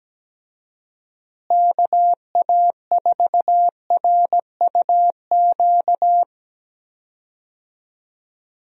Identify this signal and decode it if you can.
Morse code, heard as a keyed tone, about 17 wpm, 700 Hz. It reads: KA4RUQ
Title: USA Call Signs - 6 Characters - Repeated 2x - 17wpm